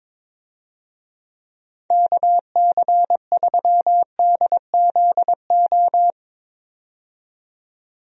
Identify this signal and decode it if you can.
Morse code, heard as a keyed tone, about 22 wpm, 700 Hz. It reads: KC3DZO
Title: USA Call Signs - 6 Characters - 22wpm